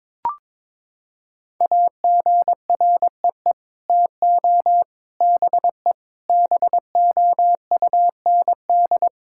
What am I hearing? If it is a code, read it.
Morse code, heard as a keyed tone, about 22 wpm, 700 Hz. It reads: AGREE TO BE BOUND
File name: Sets of 9 Words - 22wpm